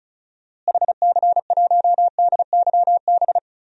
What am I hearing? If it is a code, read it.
Morse code, heard as a keyed tone, about 36 wpm, 700 Hz. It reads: HC1DYB
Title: DXCC-100 Call Signs - 6 Characters - Repeated 2x - 35wpm